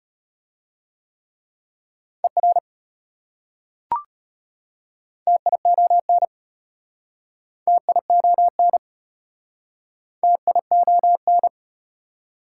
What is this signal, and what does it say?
Morse code, heard as a keyed tone, about 33 wpm, 700 Hz. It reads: ER TION TION TION